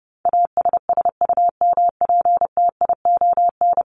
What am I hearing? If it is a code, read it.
Morse code, heard as a keyed tone, about 30 wpm, 700 Hz. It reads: ASSUMPTION
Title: Top 1751-2000 Words 30wpm